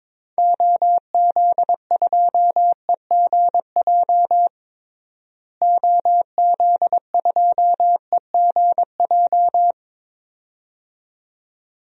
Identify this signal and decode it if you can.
Morse code, heard as a keyed tone, about 22 wpm, 700 Hz. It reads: OZ2EGJ OZ2EGJ
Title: DXCC-100 Call Signs - 6 Characters - Repeated 2x - 22wpm